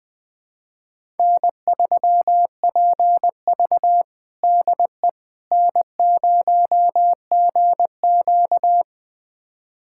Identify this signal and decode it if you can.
Morse code, heard as a keyed tone, about 20 wpm, 700 Hz. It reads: N3PV DE N0GQ
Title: USA Callsign DE Callsign - 4 Characters - 20wpm